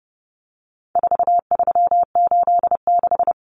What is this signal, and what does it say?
Morse code, heard as a keyed tone, about 30 wpm, 700 Hz. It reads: 4386
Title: Numbers - 4-Digit 30wpm